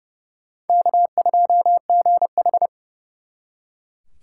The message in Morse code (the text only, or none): K2GH